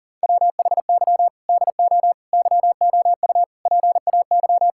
Morse code, heard as a keyed tone, about 40 wpm, 700 Hz. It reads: WHY DO YOU PAY
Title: Sentences from Top 1000 Words - Review Part 6 - 40wpm